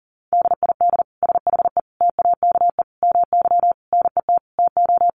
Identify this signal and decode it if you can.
Morse code, heard as a keyed tone, about 40 wpm, 700 Hz. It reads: DID SHE TAKE MY NET TO
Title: Sentences from Top 1400 Words 40wpm